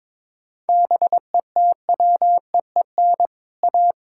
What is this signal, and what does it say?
Morse code, heard as a keyed tone, about 22 wpm, 700 Hz. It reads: BETWEEN A
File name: practice / 22wpm / Sets of 7 Words - 22wpm